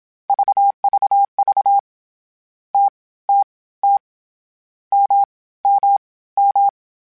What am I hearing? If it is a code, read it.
Morse code, heard as a keyed tone, about 27 wpm, spaced as for 18 wpm, 800 Hz. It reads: VVV TTT MMM
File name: Short Runs / Short Runs-mp3s / All DAHs @25